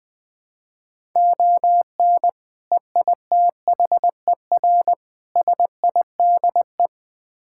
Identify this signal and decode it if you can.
Morse code, heard as a keyed tone, about 20 wpm, 700 Hz. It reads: ON EITHER SIDE
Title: Build A Phrase - 5-Word Phrases - 20wpm